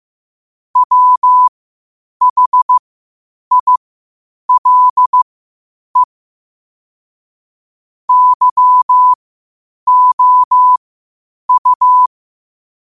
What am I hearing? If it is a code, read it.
Morse code, heard as a keyed tone, about 15 wpm, spaced as for 8 wpm, 1000 Hz. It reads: WHILE YOU
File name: qotdfiles / 8wpm